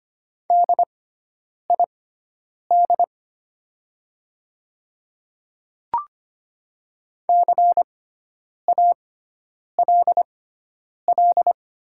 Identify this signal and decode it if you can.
Morse code, heard as a keyed tone, about 25 wpm, spaced as for 9 wpm, 700 Hz. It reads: DID CALL
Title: 6x Character Spacing - Top 100 Words - 25wpm